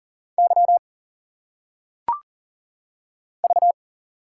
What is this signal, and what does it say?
Morse code, heard as a keyed tone, about 40 wpm, 700 Hz. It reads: Y V
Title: Single Letters - Phonetic Pronunciation 40wpm